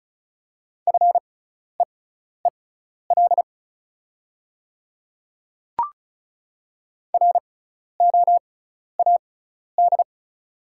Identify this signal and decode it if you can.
Morse code, heard as a keyed tone, about 36 wpm, spaced as for 12 wpm, 690 Hz. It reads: FEEL ROAD
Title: 6x Character Spacing - Top 500 Words - 35wpm